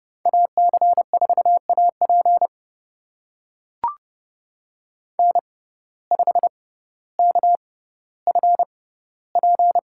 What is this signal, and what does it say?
Morse code, heard as a keyed tone, about 30 wpm, spaced as for 15 wpm, 700 Hz. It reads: AC4AP N5KFP
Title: USA Call Signs - 5 Characters - 6x Character Spacing - 30wpm